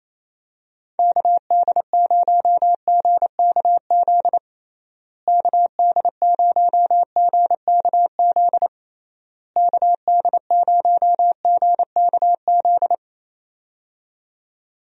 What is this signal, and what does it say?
Morse code, heard as a keyed tone, about 28 wpm, 700 Hz. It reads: KD0GKZ KD0GKZ KD0GKZ
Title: USA Call Signs - 6 Characters - Repeated 3x - 28wpm